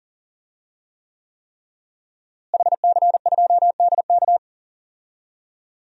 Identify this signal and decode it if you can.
Morse code, heard as a keyed tone, about 40 wpm, 700 Hz. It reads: HC2DK
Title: DXCC-100 Call Signs - 5 Characters - Repeated 3x - 40wpm